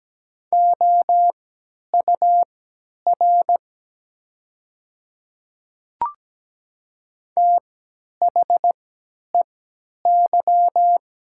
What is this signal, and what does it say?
Morse code, heard as a keyed tone, about 17 wpm, spaced as for 10 wpm, 700 Hz. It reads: OUR THEY
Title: 3x Character Spacing - Top 200 Words - 17wpm